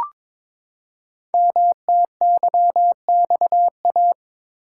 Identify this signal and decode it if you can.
Morse code, heard as a keyed tone, about 22 wpm, 700 Hz. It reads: MTYXA
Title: Code Groups - 5 Letters - 22wpm